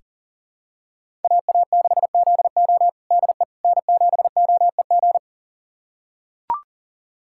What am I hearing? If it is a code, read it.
Morse code, heard as a keyed tone, about 40 wpm, 700 Hz. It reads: AA6ZO DE N7OEG
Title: USA Callsign DE Callsign - 5 Characters - 40wpm